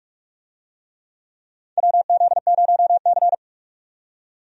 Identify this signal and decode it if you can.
Morse code, heard as a keyed tone, about 44 wpm, 700 Hz. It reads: WZ0C
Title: USA Call Signs - 4 Characters - 45wpm